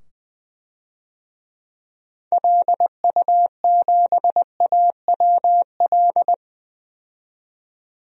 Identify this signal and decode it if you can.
Morse code, heard as a keyed tone, about 20 wpm, 700 Hz. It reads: LU7AWL
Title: DXCC-100 Call Signs - 6 Characters - 20wpm